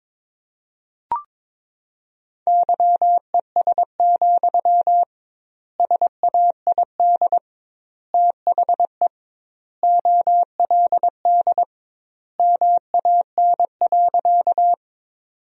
Morse code, heard as a keyed tone, about 21 wpm, 700 Hz. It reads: YES, SAID THE OLD MAN.
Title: 2x Word-Spacing - Sentences from Top 500 Words - 22wpm